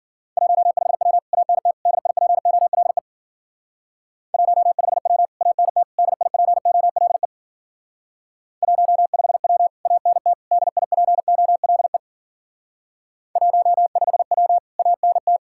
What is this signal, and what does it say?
Morse code, heard as a keyed tone, about 52 wpm, 700 Hz. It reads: 15W ANT DIPOLE 15W ANT DIPOLE 15W ANT DIPOLE 15W ANT